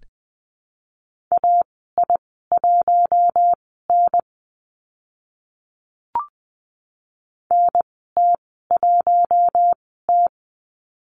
Morse code, heard as a keyed tone, about 20 wpm, spaced as for 14 wpm, 700 Hz. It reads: AI1N NT1T
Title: 042 Lesson 8g - Number 1 - Call Signs 20wpm